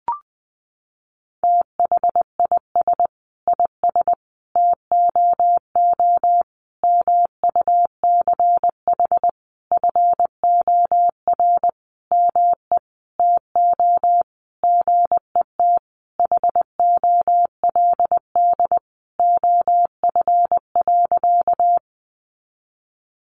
Morse code, heard as a keyed tone, about 20 wpm, 700 Hz. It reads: THIS IS TOO MUCH FOR ME TO GET HOLD OF.
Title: A Sound Of Thunder 20wpm